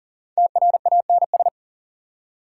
Morse code, heard as a keyed tone, about 40 wpm, 700 Hz. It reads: TRANS
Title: Prefixes and Suffixes 40wpm